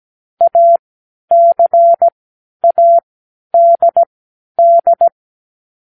Morse code, heard as a keyed tone, about 17 wpm, spaced as for 10 wpm, 680 Hz. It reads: ACADD
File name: k7qo / Lesson9(A-F)